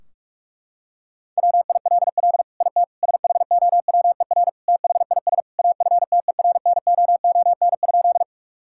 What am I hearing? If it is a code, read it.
Morse code, heard as a keyed tone, about 41 wpm, 700 Hz. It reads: WILL IT SHOWER THIS AFTERNOON?
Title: Sentences from Top 1600 Words 45wpm